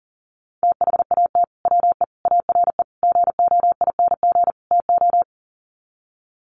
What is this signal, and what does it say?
Morse code, heard as a keyed tone, about 40 wpm, 700 Hz. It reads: THAT WE ARE GOING TO
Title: Sets of 5 Words Encore 40wpm